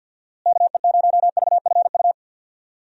Morse code, heard as a keyed tone, about 50 wpm, 700 Hz. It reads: KE0VFU
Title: USA Call Signs - 6 Characters - Repeated 2x - 50wpm